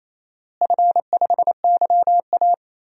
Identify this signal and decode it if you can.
Morse code, heard as a keyed tone, about 28 wpm, 700 Hz. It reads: F5YA
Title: DXCC-100 Call Signs - 4 Characters - Repeated 3x - 28wpm